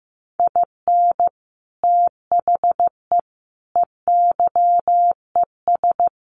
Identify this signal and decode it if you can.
Morse code, heard as a keyed tone, about 15 wpm, 700 Hz. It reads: IN THE EYES
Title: Sets of 5 Words 15wpm